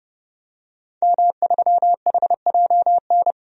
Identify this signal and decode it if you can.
Morse code, heard as a keyed tone, about 30 wpm, 700 Hz. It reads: M3HJN
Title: DXCC-100 Call Signs - 5 Characters - 30wpm